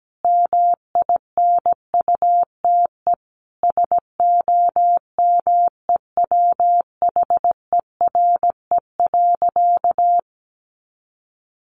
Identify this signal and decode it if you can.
Morse code, heard as a keyed tone, about 17 wpm, 700 Hz. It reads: MINUTE SOMEWHERE.
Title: Sentences from Top 1400 Words 17wpm